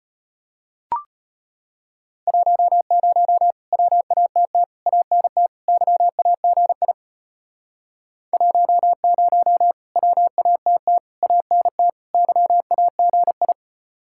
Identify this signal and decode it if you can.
Morse code, heard as a keyed tone, about 33 wpm, 700 Hz. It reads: 10 WATT ANT YAGI 10 WATT ANT YAGI